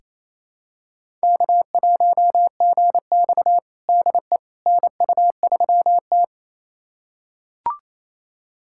Morse code, heard as a keyed tone, about 28 wpm, 700 Hz. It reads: K1GX DE NU3T
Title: USA Callsign DE Callsign - 4 Characters - 28wpm